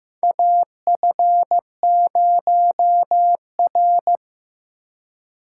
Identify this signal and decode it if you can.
Morse code, heard as a keyed tone, about 15 wpm, 700 Hz. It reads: AF0R
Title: USA Call Signs - 3x Character Spacing - 15wpm